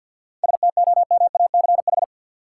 Morse code, heard as a keyed tone, about 49 wpm, 700 Hz. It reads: STOMACH
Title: Seven Letter Words - 50wpm